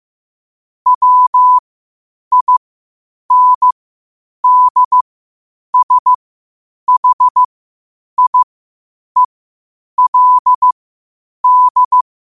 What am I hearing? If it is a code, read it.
Morse code, heard as a keyed tone, about 15 wpm, spaced as for 8 wpm, 1000 Hz. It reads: WINDSHIELD
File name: qotdfiles / 8wpm